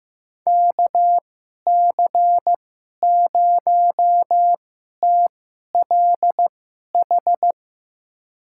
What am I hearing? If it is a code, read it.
Morse code, heard as a keyed tone, about 15 wpm, spaced as for 11 wpm, 700 Hz. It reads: KC0TLH
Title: USA Call Signs - 6 Characters - 2x Character Spacing - 15wpm